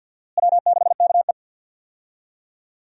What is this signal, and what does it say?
Morse code, heard as a keyed tone, about 52 wpm, 700 Hz. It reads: W6KE
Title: USA Call Signs - 4 Characters - Repeated 3x - 50wpm